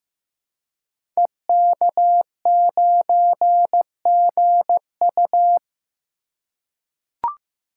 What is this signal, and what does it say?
Morse code, heard as a keyed tone, about 15 wpm, 700 Hz. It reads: EK9GU
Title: Code Groups - 5 Letter:Numbers - 15wpm